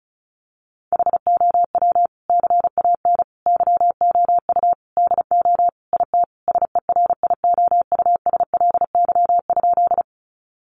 Morse code, heard as a keyed tone, about 34 wpm, 700 Hz. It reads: HOW CAN YOU DO IT SERIOUSLY?
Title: Sentences from Top 1200 Words 35wpm